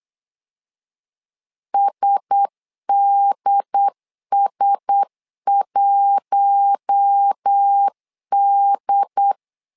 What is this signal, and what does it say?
Morse code, heard as a keyed tone, about 8 wpm, 800 Hz. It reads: SDS1D
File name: LEZIONI / 4 Q1S 8wpm 2710